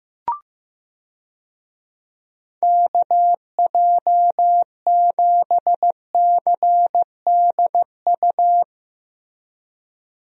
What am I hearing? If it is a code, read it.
Morse code, heard as a keyed tone, about 15 wpm, 700 Hz. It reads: KJ7CDU
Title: USA Call Signs - 6 Characters - 15wpm